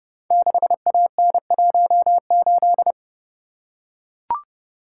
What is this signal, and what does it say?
Morse code, heard as a keyed tone, about 28 wpm, 700 Hz. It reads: 6AN18
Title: Code Groups - 5 Letter:Numbers - 30wpm